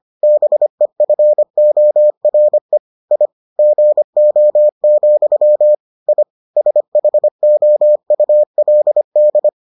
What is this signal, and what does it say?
Morse code, heard as a keyed tone, about 25 wpm, 600 Hz. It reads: BEFORE I GO, I SHOULD